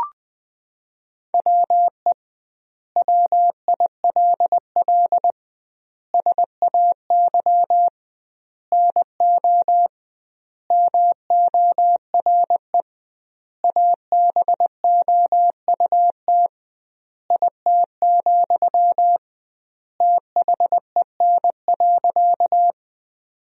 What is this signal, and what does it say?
Morse code, heard as a keyed tone, about 20 wpm, 700 Hz. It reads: WE WILL SAY NO MORE ABOUT IT, THEN.
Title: 2x Word-Spacing - Sentences from Top 200 Words - 20wpm